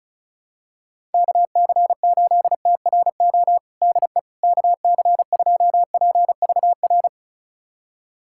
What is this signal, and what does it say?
Morse code, heard as a keyed tone, about 36 wpm, 700 Hz. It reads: KC8TRO DE KC2PVR
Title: USA Callsign DE Callsign - 6 Characters - 35wpm